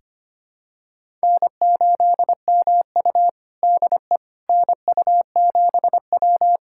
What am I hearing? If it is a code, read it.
Morse code, heard as a keyed tone, about 25 wpm, 700 Hz. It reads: N8MU DE NU7W
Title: USA Callsign DE Callsign - 4 Characters - 25wpm